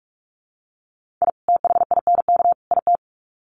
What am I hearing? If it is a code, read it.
Morse code, heard as a keyed tone, about 44 wpm, 700 Hz. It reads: I THINK IT
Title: Sets of 3 Words 45wpm